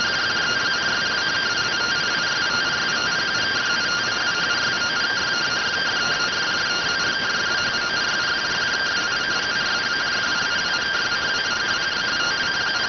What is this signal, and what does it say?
Two-tone teleprinter signal, about 170 Hz shift.